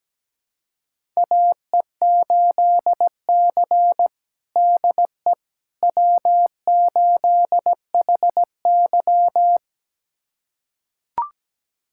Morse code, heard as a keyed tone, about 17 wpm, 700 Hz. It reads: AE8C DE W8HY
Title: USA Callsign DE Callsign - 4 Characters - 17wpm